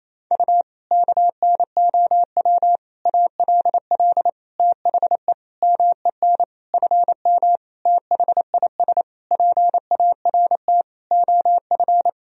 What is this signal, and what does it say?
Morse code, heard as a keyed tone, about 28 wpm, 700 Hz. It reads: U KNOW ALL THE MEN FM THIS PART OF
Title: Sentences from Top 300 Words and Abbreviations 28wpm